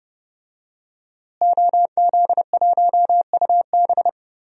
Morse code, heard as a keyed tone, about 30 wpm, 700 Hz. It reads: OZ1UB
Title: DXCC-100 Call Signs - 5 Characters - Repeated 2x - 30wpm